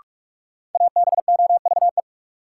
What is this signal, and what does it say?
Morse code, heard as a keyed tone, about 48 wpm, 700 Hz. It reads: ABOVE
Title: Sentences from Top 1200 Words - New Word Drill Alt - 45wpm